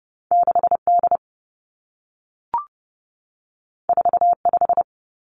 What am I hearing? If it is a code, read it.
Morse code, heard as a keyed tone, about 30 wpm, 700 Hz. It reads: 6 D 4 5
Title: Two Challenging Letter-Number Combinations 30wpm